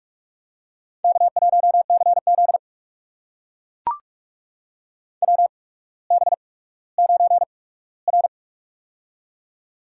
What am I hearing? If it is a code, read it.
Morse code, heard as a keyed tone, about 44 wpm, spaced as for 19 wpm, 690 Hz. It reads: K1CZ WB9R
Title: USA Call Signs - 4 Characters - 8x Character Spacing - 45wpm